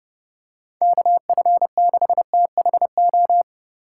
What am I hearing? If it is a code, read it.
Morse code, heard as a keyed tone, about 30 wpm, 700 Hz. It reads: KF6THO